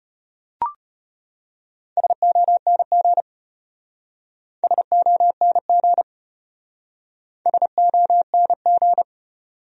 Morse code, heard as a keyed tone, about 30 wpm, 700 Hz. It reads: SONG SONG SONG
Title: Speed-Racing Top 500 Words 25wpm